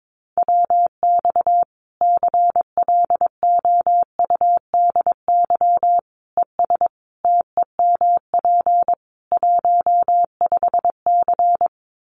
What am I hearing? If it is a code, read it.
Morse code, heard as a keyed tone, about 22 wpm, 700 Hz. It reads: WX CLOUDY ES TEMP 15C
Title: Common QSO Phrases 22wpm